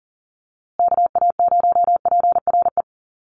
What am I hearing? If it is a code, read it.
Morse code, heard as a keyed tone, about 40 wpm, 700 Hz. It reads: KA0PRE
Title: USA Call Signs Repeated 3x 40wpm